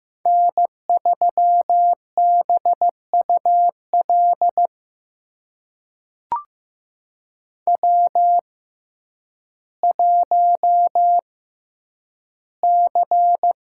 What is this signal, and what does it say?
Morse code, heard as a keyed tone, about 15 wpm, spaced as for 9 wpm, 700 Hz. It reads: N3BUL W1C
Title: USA Call Signs - 5 Characters - 6x Character Spacing - 15wpm